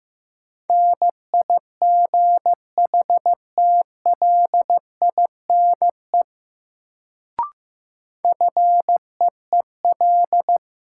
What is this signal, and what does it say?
Morse code, heard as a keyed tone, about 15 wpm, 700 Hz. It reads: NIGHTLINE FEEL
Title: Sets of 5 Words - New Word Drill - 15wpm